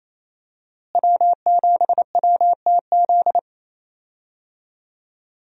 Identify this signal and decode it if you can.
Morse code, heard as a keyed tone, about 27 wpm, 700 Hz. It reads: W7WTZ
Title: USA Call Signs - 5 Characters - Repeated 3x - 28wpm